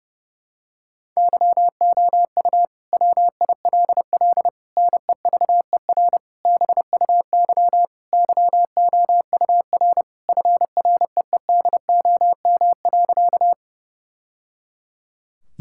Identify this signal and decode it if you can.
Morse code, heard as a keyed tone, about 32 wpm, 700 Hz. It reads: YOU WILL NEVER BUY YOUR FREEDOM.
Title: Sentences from Top 1000 Words - Review Part 5 - 30wpm